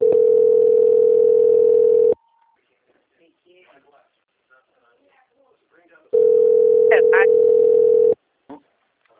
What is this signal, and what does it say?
Ringback tone (North America), 440 + 480 Hz, with speech.
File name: out- - - - - .57648